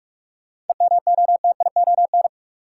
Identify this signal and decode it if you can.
Morse code, heard as a keyed tone, about 46 wpm, 700 Hz. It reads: EMOTION
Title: Sentences from Top 1600 Words - New Word Drill Alt - 45wpm